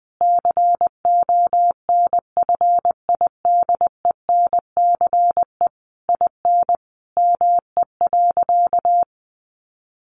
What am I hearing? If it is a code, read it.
Morse code, heard as a keyed tone, about 20 wpm, 700 Hz. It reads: CONFIDENCE IN ME.
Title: Sentences from Top 1500 Words 20wpm